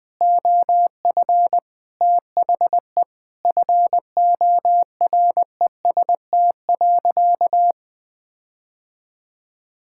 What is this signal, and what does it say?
Morse code, heard as a keyed tone, about 20 wpm, 700 Hz. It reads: OF THE FOREST.
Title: Sentences with 5-Word Phrases - 20wpm